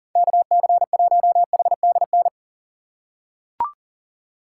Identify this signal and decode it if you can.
Morse code, heard as a keyed tone, about 40 wpm, 700 Hz. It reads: KC1HDN
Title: USA Call Signs - 6 Characters - 6x Character Spacing - 40wpm